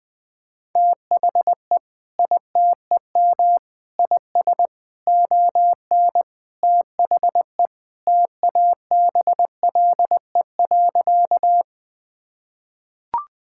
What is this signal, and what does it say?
Morse code, heard as a keyed tone, about 21 wpm, 700 Hz. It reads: THE ITEM IS ON THE TABLE.
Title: Sentences from Top 1000 Words - Review Part 6 - 20wpm